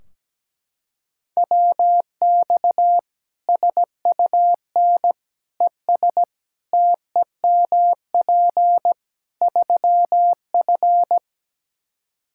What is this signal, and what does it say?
Morse code, heard as a keyed tone, about 17 wpm, 700 Hz. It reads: WX SUN ES TEMP 3F
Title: K1USN SST QSO Practice 17wpm